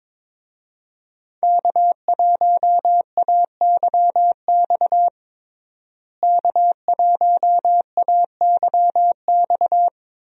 Morse code, heard as a keyed tone, about 22 wpm, 700 Hz. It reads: K1AYX K1AYX
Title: USA Call Signs - 5 Characters - Repeated 3x - 22wpm